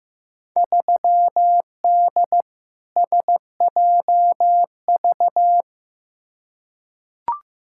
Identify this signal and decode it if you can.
Morse code, heard as a keyed tone, about 15 wpm, 700 Hz. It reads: 3D SJV
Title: Field Day Exchanges - 15wpm